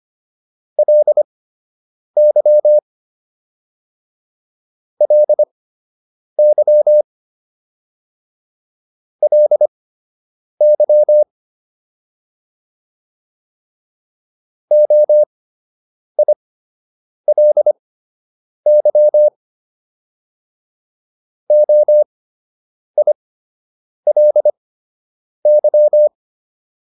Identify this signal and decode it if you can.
Morse code, heard as a keyed tone, about 25 wpm, spaced as for 8 wpm, 600 Hz. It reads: LY LY LY OILY OILY